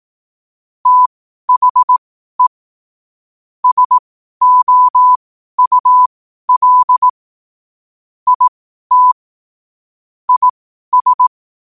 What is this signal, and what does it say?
Morse code, heard as a keyed tone, about 18 wpm, spaced as for 12 wpm, 1000 Hz. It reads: THE SOUL IT IS
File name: qotdfiles / 12wpm